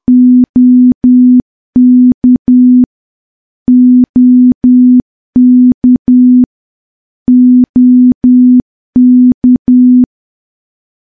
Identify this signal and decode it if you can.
Morse code, heard as a keyed tone, about 10 wpm, 250 Hz. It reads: OK OK OK
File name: morse / morse-training-10WPM